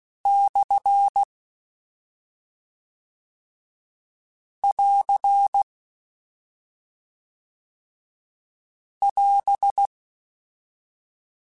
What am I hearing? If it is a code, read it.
Morse code, heard as a keyed tone, about 16 wpm, 800 Hz. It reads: / <AR> <AS>
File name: Short Runs / Short Runs-mp3s / Words @5